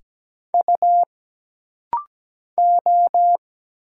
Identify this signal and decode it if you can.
Morse code, heard as a keyed tone, about 17 wpm, 700 Hz. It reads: U O
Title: Single Letters - Rapid-Fire 17wpm